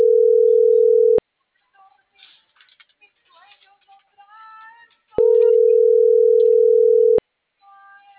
Ringback tone (North America), 440 + 480 Hz.
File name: CENTRAL NMP / out melanie.diaznmp